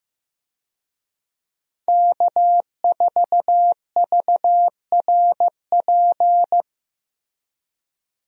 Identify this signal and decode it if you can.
Morse code, heard as a keyed tone, about 15 wpm, 700 Hz. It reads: K4VRP
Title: USA Call Signs - 5 Characters - Repeated 2x - 15wpm